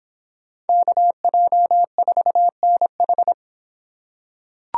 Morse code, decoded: KJ4NH

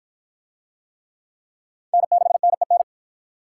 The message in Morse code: N6NEN

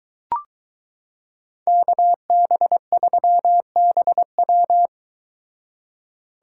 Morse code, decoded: KB3BW

